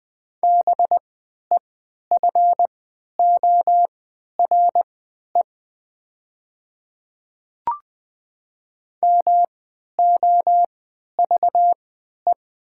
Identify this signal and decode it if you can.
Morse code, heard as a keyed tone, about 20 wpm, spaced as for 11 wpm, 700 Hz. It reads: BEFORE MOVE
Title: 3x Character Spacing - Top 200 Words - 20wpm